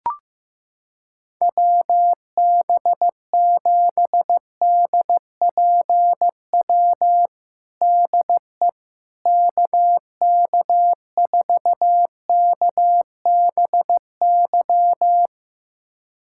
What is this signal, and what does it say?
Morse code, heard as a keyed tone, about 15 wpm, 700 Hz. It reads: WB7DPW DE KK4KBY
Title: USA Callsign DE Callsign - 6 Characters - 15wpm